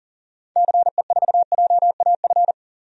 Morse code, heard as a keyed tone, about 37 wpm, 700 Hz. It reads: CE4JAF
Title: DXCC-100 Call Signs - 6 Characters - 40wpm